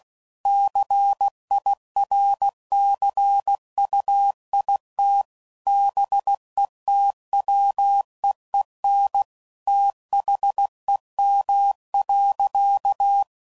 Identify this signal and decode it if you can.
Morse code, heard as a keyed tone, about 16 wpm, 800 Hz. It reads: CIRCUIT BETWEEN THEM.
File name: CW Code Groups @15WPM / Technical Subjects / PwrSupply